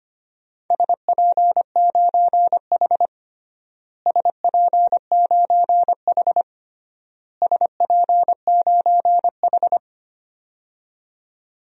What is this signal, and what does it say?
Morse code, heard as a keyed tone, about 25 wpm, 700 Hz. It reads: SP9H SP9H SP9H
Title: DXCC-100 Call Signs - 4 Characters - Repeated 3x - 25wpm